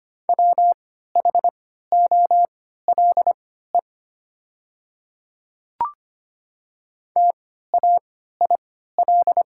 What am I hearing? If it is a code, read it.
Morse code, heard as a keyed tone, about 25 wpm, spaced as for 14 wpm, 700 Hz. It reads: WHOLE TAIL
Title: 3x Character Spacing - Top 500 Words - 25wpm